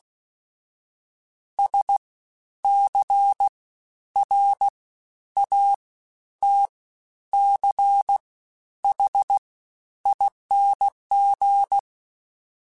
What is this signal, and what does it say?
Morse code, heard as a keyed tone, about 16 wpm, 800 Hz. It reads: S C R A T C H ING
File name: Special INGs-mp3s / Special INGs-mp3s / SlowToFaster ing @15